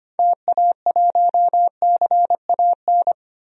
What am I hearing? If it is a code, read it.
Morse code, heard as a keyed tone, about 25 wpm, 700 Hz. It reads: TA1CAN
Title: DXCC-100 Call Signs - 6 Characters - 25wpm